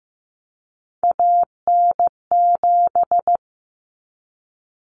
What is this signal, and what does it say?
Morse code, heard as a keyed tone, about 15 wpm, 700 Hz. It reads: AN7
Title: Common Cut Numbers - 3-Digit 15wpm